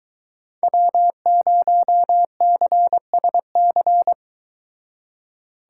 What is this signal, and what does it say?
Morse code, heard as a keyed tone, about 22 wpm, 700 Hz. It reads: W0CSC